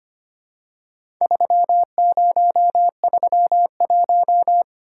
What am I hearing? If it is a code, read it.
Morse code, heard as a keyed tone, about 25 wpm, 700 Hz. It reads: 3031
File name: practice / 25wpm / Four Letter-Number Combinations - 25wpm